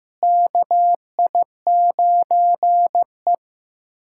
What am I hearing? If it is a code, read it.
Morse code, heard as a keyed tone, about 15 wpm, 700 Hz. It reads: KI9E
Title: USA Callsign DE Callsign - 4 Characters - 15wpm